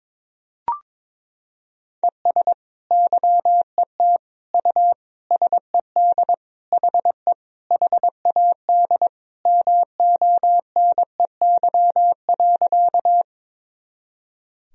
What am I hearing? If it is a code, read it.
Morse code, heard as a keyed tone, about 22 wpm, 700 Hz. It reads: ES YET U SED HE HAD MONEY.
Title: Sentences from Top 500 Words and Abbreviations 22wpm